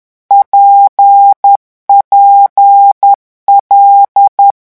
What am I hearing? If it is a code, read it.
Morse code, heard as a keyed tone, about 10 wpm, 800 Hz. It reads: PPL
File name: morse mp3 / lecon6-vitesse 10